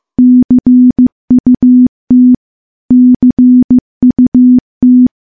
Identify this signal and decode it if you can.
Morse code, heard as a keyed tone, about 15 wpm, 250 Hz. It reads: CUT CUT
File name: morse / morse-training-15WPM